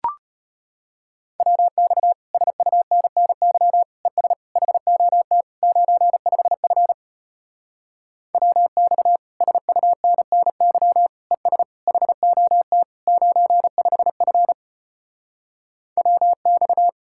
Morse code, heard as a keyed tone, about 33 wpm, 700 Hz. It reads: WX SUNNY ES HOT 95F WX SUNNY ES HOT 95F WX